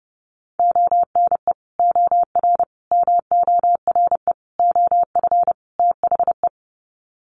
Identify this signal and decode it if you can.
Morse code, heard as a keyed tone, about 30 wpm, 700 Hz. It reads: ONE OR MORE OF THE
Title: Sets of 5 Words Encore 30wpm